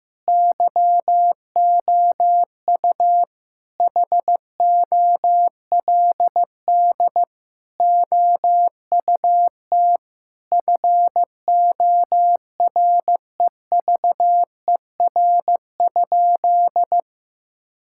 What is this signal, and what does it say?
Morse code, heard as a keyed tone, about 15 wpm, 700 Hz. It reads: YOU HOLD OUT FOREVER?
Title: Sentences from Top 1600 Words 15wpm